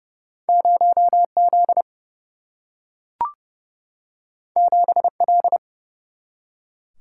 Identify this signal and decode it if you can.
Morse code, heard as a keyed tone, about 29 wpm, 700 Hz. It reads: 0Z 7L
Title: Single-Letter Single-Number 30wpm